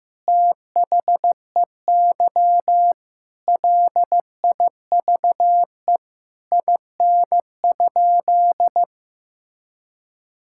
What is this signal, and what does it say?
Morse code, heard as a keyed tone, about 15 wpm, 700 Hz. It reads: THEY LIVE IN?
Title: Sentences from Top 1000 Words - Review Part 5 - 15wpm